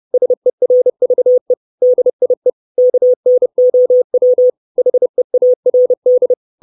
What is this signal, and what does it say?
Morse code, heard as a keyed tone, about 30 wpm, 500 Hz. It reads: SERVE DIE KNOW HEARD
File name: 632 Words 30WPM